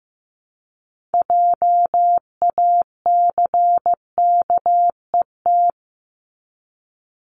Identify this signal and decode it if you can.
Morse code, heard as a keyed tone, about 15 wpm, 700 Hz. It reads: JACKET
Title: Top 1751-2000 Words 15wpm